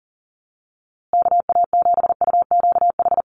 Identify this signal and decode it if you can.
Morse code, heard as a keyed tone, about 40 wpm, 700 Hz. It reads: KA7UQH
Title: USA Call Signs Repeated 2x 40wpm